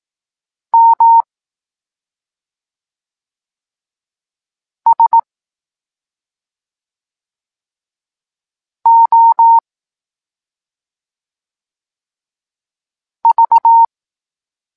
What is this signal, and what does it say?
Morse code, heard as a keyed tone, about 18 wpm, 920 Hz. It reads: M S O V
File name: Tekst-9-rownomierny